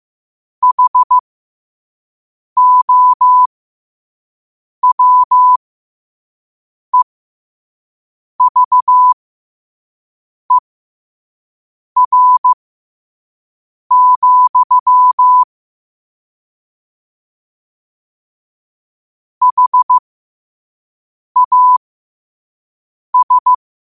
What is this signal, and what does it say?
Morse code, heard as a keyed tone, about 15 wpm, spaced as for 5 wpm, 1000 Hz. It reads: HOWEVER, HAS
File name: qotdfiles / 5wpm